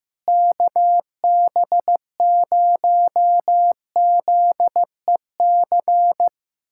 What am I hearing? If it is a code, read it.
Morse code, heard as a keyed tone, about 15 wpm, 700 Hz. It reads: KB0ZEC
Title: USA Callsign DE Callsign - 6 Characters - 15wpm